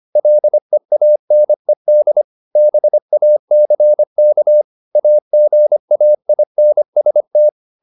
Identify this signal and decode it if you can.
Morse code, heard as a keyed tone, about 25 wpm, 600 Hz. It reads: LEANED BACK AGAINST